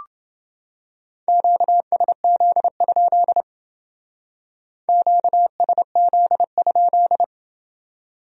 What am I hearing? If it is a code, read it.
Morse code, heard as a keyed tone, about 31 wpm, 700 Hz. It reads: QSZ? QSZ?